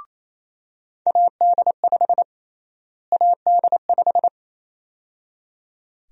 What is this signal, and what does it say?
Morse code, heard as a keyed tone, about 27 wpm, 700 Hz. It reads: AD5 AD5
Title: USA Call Sign Prefixes - Repeated 2x - 28wpm